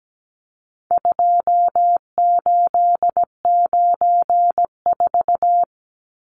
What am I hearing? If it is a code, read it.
Morse code, heard as a keyed tone, about 17 wpm, 700 Hz. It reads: 2894
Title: Numbers - 4-Digit 17wpm